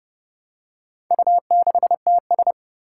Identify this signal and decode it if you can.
Morse code, heard as a keyed tone, about 30 wpm, 700 Hz. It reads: U6TS
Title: Four Challenging Letter-Number Combinations 30wpm